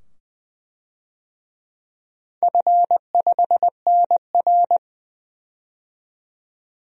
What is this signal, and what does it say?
Morse code, heard as a keyed tone, about 20 wpm, 700 Hz. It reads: F5NR
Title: DXCC-100 Call Signs - 4 Characters - 20wpm